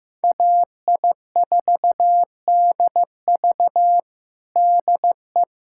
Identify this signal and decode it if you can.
Morse code, heard as a keyed tone, about 15 wpm, 700 Hz. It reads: AI4DV DE
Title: USA Callsign DE Callsign - 5 Characters - 15wpm